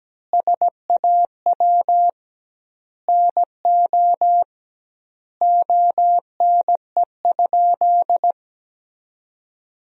Morse code, heard as a keyed tone, about 17 wpm, 700 Hz. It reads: SAW NO ONE?
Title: 2x Word-Spacing - Sentences from Top 300 Words - 17wpm